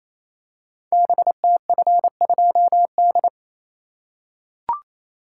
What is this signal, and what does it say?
Morse code, heard as a keyed tone, about 29 wpm, 700 Hz. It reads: BTF2D
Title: Code Groups - 5 Letter:Numbers - 28wpm